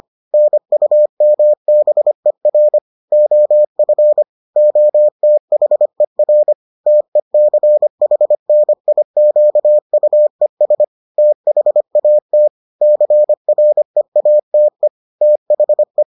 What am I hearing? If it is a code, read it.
Morse code, heard as a keyed tone, about 25 wpm, 600 Hz. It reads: NUMBER OF OTHER TECHNIQUES THAT CREATE THE